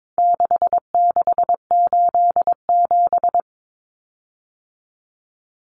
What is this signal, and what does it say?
Morse code, heard as a keyed tone, about 22 wpm, 700 Hz. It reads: 6687
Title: Common Cut Numbers - 4-Digit 22wpm